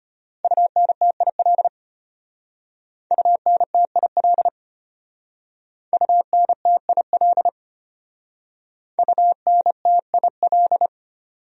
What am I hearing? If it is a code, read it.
Morse code, heard as a keyed tone, about 32 wpm, 700 Hz. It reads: UNTIL UNTIL UNTIL UNTIL